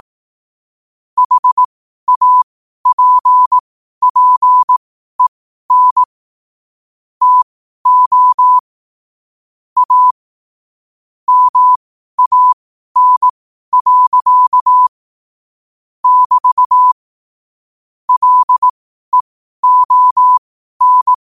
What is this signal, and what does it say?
Morse code, heard as a keyed tone, about 18 wpm, spaced as for 12 wpm, 1000 Hz. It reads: HAPPEN TO A MAN. <BT> LEON
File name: qotdfiles / 12wpm